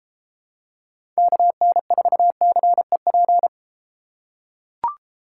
Morse code, heard as a keyed tone, about 32 wpm, 700 Hz. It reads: KN4CEP